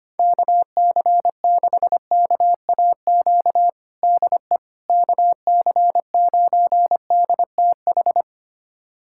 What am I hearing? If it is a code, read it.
Morse code, heard as a keyed tone, about 25 wpm, 700 Hz. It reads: KC6KAQ DE KC9DTH
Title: USA Callsign DE Callsign - 6 Characters - 25wpm